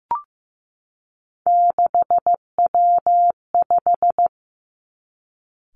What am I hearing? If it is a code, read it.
Morse code, heard as a keyed tone, about 15 wpm, 700 Hz. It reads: 6W5
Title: Three Challenging Letter-Number Combinations 15wpm